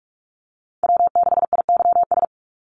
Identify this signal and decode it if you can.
Morse code, heard as a keyed tone, about 46 wpm, 700 Hz. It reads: W6IYS
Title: NA SOTA Chaser Callsigns 45wpm